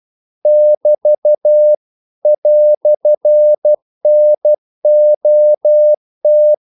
Morse code, heard as a keyed tone, about 12 wpm, 600 Hz. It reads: <BT> "NOT
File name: CW / Practice / EnglishQuotations0112